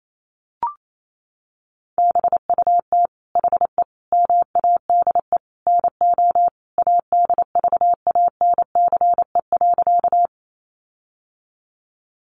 Morse code, heard as a keyed tone, about 28 wpm, 700 Hz. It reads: BUT HE MADE NO ADVANCE.
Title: Sentences from Top 1400 Words 28wpm